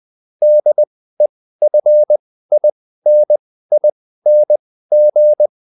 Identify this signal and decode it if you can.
Morse code, heard as a keyed tone, about 20 wpm, spaced as for 14 wpm, 600 Hz. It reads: DEFINING